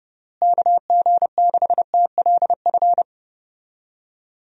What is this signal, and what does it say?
Morse code, heard as a keyed tone, about 32 wpm, 700 Hz. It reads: KG6TLF
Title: USA Call Signs - 6 Characters - Repeated 3x - 30wpm